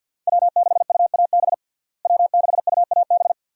Morse code, heard as a keyed tone, about 49 wpm, 700 Hz. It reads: W6UAB W6UAB
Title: USA Call Signs - 5 Characters - Repeated 2x - 50wpm